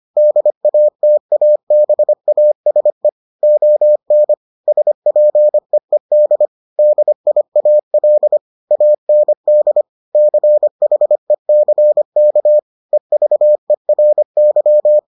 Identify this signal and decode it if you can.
Morse code, heard as a keyed tone, about 25 wpm, 600 Hz. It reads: DATABASE ON SPEED DIAL AND CHECK EVERY